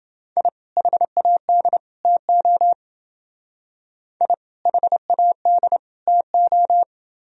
Morse code, heard as a keyed tone, about 28 wpm, 700 Hz. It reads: I HAD TO I HAD TO